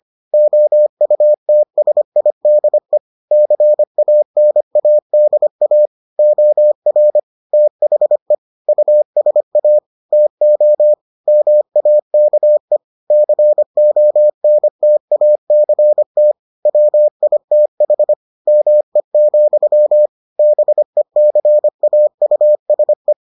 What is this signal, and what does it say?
Morse code, heard as a keyed tone, about 25 wpm, 600 Hz. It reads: OUTSIDE CANADA OR THE USA TO MAKE CONTACT WITH ME, BECAUSE